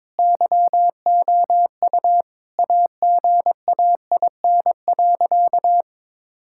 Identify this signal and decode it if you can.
Morse code, heard as a keyed tone, about 22 wpm, 700 Hz. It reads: YOU AGAIN.
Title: Sentences from Top 1600 Words 22wpm